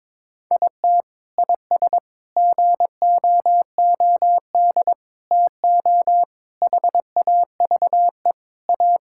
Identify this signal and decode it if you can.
Morse code, heard as a keyed tone, about 23 wpm, 700 Hz. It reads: IT IS GOOD TO HAVE A
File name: practice / 22wpm / Sentences with 3-Word Phrases - 22wpm